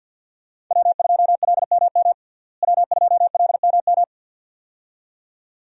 Morse code, heard as a keyed tone, about 53 wpm, 700 Hz. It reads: W2LMM W2LMM
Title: USA Call Signs - 5 Characters - Repeated 2x - 50wpm